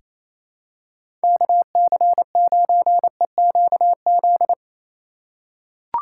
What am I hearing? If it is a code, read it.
Morse code, heard as a keyed tone, about 28 wpm, 700 Hz. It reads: KC9EQZ
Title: USA Call Signs - 6 Characters - 3x Character Spacing - 28wpm